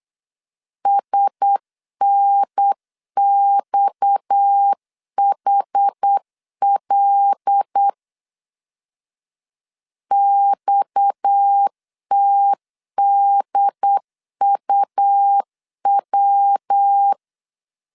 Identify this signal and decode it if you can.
Morse code, heard as a keyed tone, about 8 wpm, 800 Hz. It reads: SNXHL XTDUW
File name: CW / DET / 1 6 1500C S C 8wpm 5219